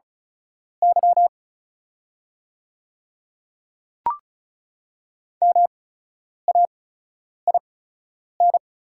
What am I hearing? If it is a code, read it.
Morse code, heard as a keyed tone, about 36 wpm, spaced as for 10 wpm, 690 Hz. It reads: Y MAIN
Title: 8x Character Spacing - Top 500 Words - 35wpm